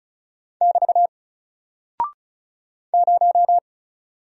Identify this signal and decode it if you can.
Morse code, heard as a keyed tone, about 36 wpm, 700 Hz. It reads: <BT> 0
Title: Single Character (Letter:Number:Punctuation:Prosign) Alt 35wpm